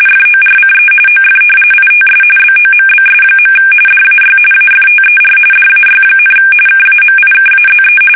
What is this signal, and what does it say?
RTTY, about 45.45 baud. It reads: FORCE OF ITS BOMBS. DAY AFTER DAY, KAMIKAZE
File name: fsk850 / battleshipnj